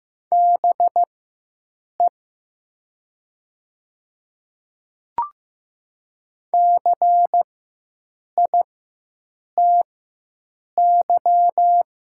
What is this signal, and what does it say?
Morse code, heard as a keyed tone, about 15 wpm, spaced as for 7 wpm, 700 Hz. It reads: BE CITY
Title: 4x Character Spacing - Top 300 Words - 15wpm